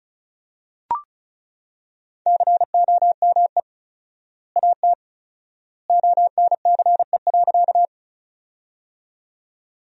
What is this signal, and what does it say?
Morse code, heard as a keyed tone, about 34 wpm, 700 Hz. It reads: COME AT ONCE.
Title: 4x Word-Spacing - Sentences from Top 300 Words - 35wpm